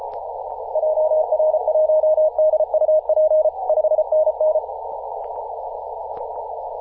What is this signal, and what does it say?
Morse code, heard as a keyed tone, about 34 wpm, 620 Hz. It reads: JR1DUP 5NN
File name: posts / 2017-aug-07-0652z-ri0li-14-036mhz-cw